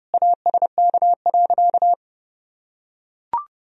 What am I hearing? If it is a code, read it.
Morse code, heard as a keyed tone, about 32 wpm, 700 Hz. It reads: ASK.
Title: Sentences from Top 1000 Words - Review Part 1 - 30wpm